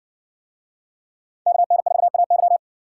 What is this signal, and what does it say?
Morse code, heard as a keyed tone, about 60 wpm, 700 Hz. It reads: KN4AY